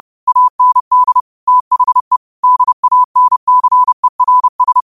Morse code, heard as a keyed tone, about 30 wpm, 1000 Hz. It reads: AND THE DANCERS